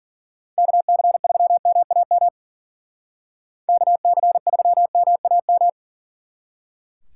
Morse code, heard as a keyed tone, about 48 wpm, 700 Hz. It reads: KC3MAM KC3MAM
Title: Speed Racing - USA 2x3 Call Signs 40wpm